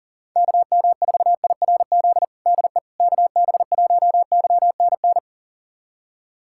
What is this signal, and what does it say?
Morse code, heard as a keyed tone, about 37 wpm, 700 Hz. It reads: KM4IRZ DE KB1YNN
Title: USA Callsign DE Callsign - 6 Characters - 40wpm